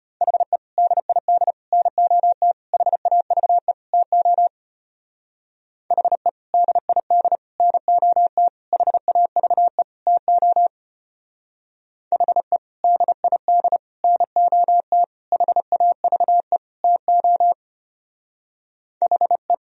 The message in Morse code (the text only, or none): HE DID NOT HAVE TO HE DID NOT HAVE TO HE DID NOT HAVE TO HE